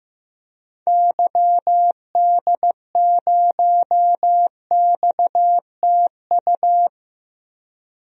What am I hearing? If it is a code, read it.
Morse code, heard as a keyed tone, about 15 wpm, 700 Hz. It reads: YD0XTU
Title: DXCC-100 Call Signs - 6 Characters - 15wpm